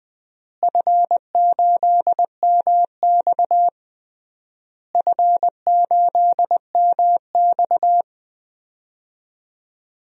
Morse code, heard as a keyed tone, about 21 wpm, 700 Hz. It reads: F8MX F8MX
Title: DXCC-100 Call Signs - 4 Characters - Repeated 3x - 20wpm